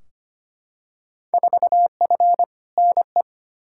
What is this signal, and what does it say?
Morse code, heard as a keyed tone, about 25 wpm, 700 Hz. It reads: 4F NE
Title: Field Day Exchanges Encore - 25wpm